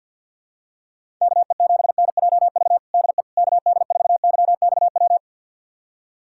Morse code, heard as a keyed tone, about 51 wpm, 700 Hz. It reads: KE7NJV DE KD4YXW
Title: USA Callsign DE Callsign 50wpm